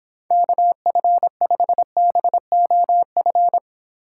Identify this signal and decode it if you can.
Morse code, heard as a keyed tone, about 25 wpm, 700 Hz. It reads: KF5BOF